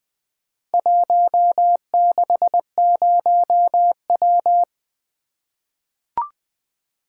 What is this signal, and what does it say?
Morse code, heard as a keyed tone, about 20 wpm, 700 Hz. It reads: 160W
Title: Four Letter-Number Combinations - 20wpm